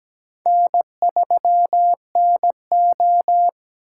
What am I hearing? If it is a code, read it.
Morse code, heard as a keyed tone, about 17 wpm, 700 Hz. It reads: N3NO
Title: USA Call Signs - 3x Character Spacing - 17wpm